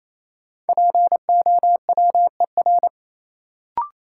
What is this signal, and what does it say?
Morse code, heard as a keyed tone, about 28 wpm, 700 Hz. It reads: POWER
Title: Sets of 4 Words - New Word Drill - 28wpm